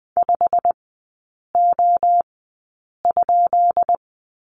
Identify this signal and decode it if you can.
Morse code, heard as a keyed tone, about 20 wpm, spaced as for 8 wpm, 700 Hz. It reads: 5O?
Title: 133 Lesson 22e - Question Mark - Three Characters 20wpm